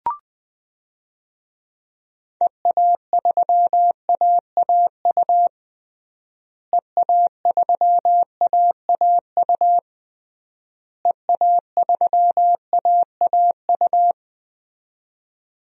Morse code, heard as a keyed tone, about 20 wpm, 700 Hz. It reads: EA3AAU EA3AAU EA3AAU
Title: DXCC-100 Call Signs - 6 Characters - Repeated 3x - 20wpm